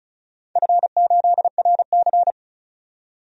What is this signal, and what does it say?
Morse code, heard as a keyed tone, about 34 wpm, 700 Hz. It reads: F8RC
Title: DXCC-100 Call Signs - 4 Characters - 35wpm